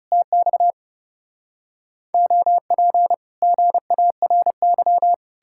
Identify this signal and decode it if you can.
Morse code, heard as a keyed tone, about 38 wpm, 700 Hz. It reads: TX OP GARY
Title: Speed-Racing Common QSO Phrases - 30wpm